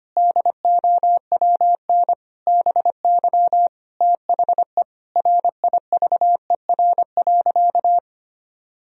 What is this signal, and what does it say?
Morse code, heard as a keyed tone, about 25 wpm, 700 Hz. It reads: DOWN BY THE RIVER.
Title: Sentences from Top 1000 Words - Review Part 4 - 25wpm